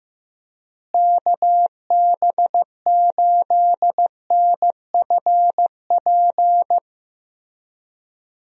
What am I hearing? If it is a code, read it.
Morse code, heard as a keyed tone, about 15 wpm, 700 Hz. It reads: KB8NFP
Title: USA Call Signs - 6 Characters - Repeated 2x - 15wpm